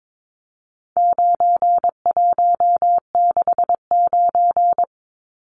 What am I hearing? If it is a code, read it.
Morse code, heard as a keyed tone, about 22 wpm, 700 Hz. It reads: 9169
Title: Numbers - 4-Digit 22wpm